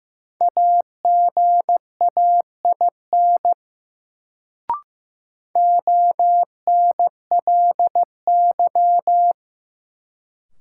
Morse code, heard as a keyed tone, about 15 wpm, 700 Hz. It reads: AGAIN ONLY
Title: Sentences from Top 200 Words - New Word Drill - 15wpm